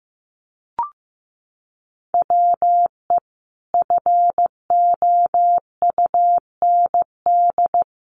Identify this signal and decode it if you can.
Morse code, heard as a keyed tone, about 15 wpm, 700 Hz. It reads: WE FOUND
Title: Sentences from Top 300 Words 15wpm